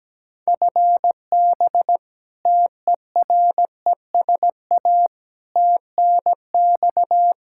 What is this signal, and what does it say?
Morse code, heard as a keyed tone, about 17 wpm, 700 Hz. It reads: FB TERESA TNX
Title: Practice QSO - Exchange 3 - More Info - 17wpm